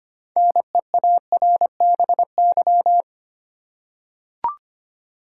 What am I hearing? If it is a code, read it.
Morse code, heard as a keyed tone, about 25 wpm, 700 Hz. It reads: NEARBY
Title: Six Letter Words - 25wpm